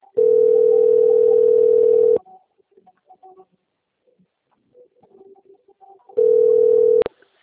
Ringback tone (North America), 440 + 480 Hz.
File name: out- - - - - .78263